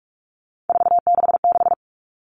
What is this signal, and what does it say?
Morse code, heard as a keyed tone, about 44 wpm, 700 Hz. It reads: 466
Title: All Cut Numbers - 3-Digit 45wpm